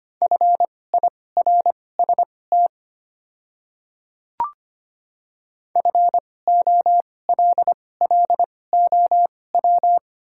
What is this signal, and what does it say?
Morse code, heard as a keyed tone, about 25 wpm, spaced as for 18 wpm, 700 Hz. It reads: FIRST FOLLOW
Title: 2x Character Spacing - Top 300 Words - 25wpm